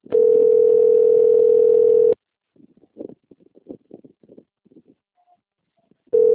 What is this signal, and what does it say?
Ringback tone (North America), 440 + 480 Hz.
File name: rg-1600-unknown- - - .12235